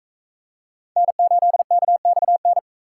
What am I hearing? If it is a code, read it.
Morse code, heard as a keyed tone, about 38 wpm, 700 Hz. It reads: N8KXN